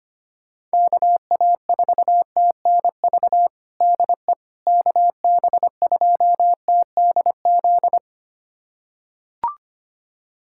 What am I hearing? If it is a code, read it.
Morse code, heard as a keyed tone, about 25 wpm, 700 Hz. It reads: KA4TNV DE KB2TDZ
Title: USA Callsign DE Callsign - 6 Characters - 25wpm